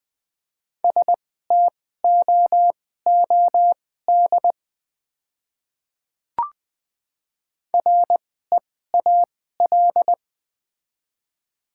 Morse code, heard as a keyed tone, about 20 wpm, spaced as for 15 wpm, 700 Hz. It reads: STOOD REAL
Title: 2x Character Spacing - Top 500 Words - 20wpm